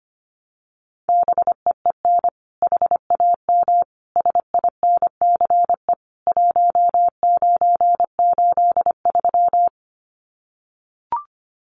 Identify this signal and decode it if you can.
Morse code, heard as a keyed tone, about 25 wpm, 700 Hz. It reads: BEEN HAM SINCE 1983
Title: Common QSO Phrases 25wpm